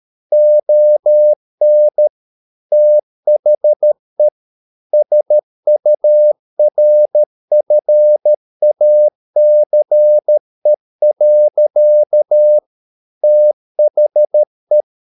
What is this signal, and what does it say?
Morse code, heard as a keyed tone, about 13 wpm, 600 Hz. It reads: ON THE SURFACE. THE